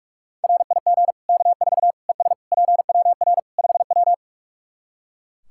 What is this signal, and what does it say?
Morse code, heard as a keyed tone, about 41 wpm, 700 Hz. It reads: RIG K4 ES PWR 5W
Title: K1USN SST QSO Practice 45wpm